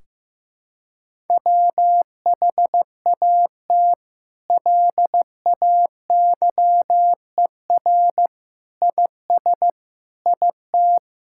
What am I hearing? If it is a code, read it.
Morse code, heard as a keyed tone, about 15 wpm, 700 Hz. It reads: WHAT LAYER IS IT
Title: Sentences from Top 1600 Words 15wpm